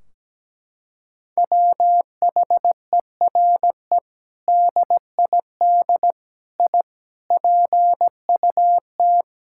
Morse code, heard as a keyed tone, about 17 wpm, 700 Hz. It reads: WHERE DID I PUT
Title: Sentences from Top 1600 Words 17wpm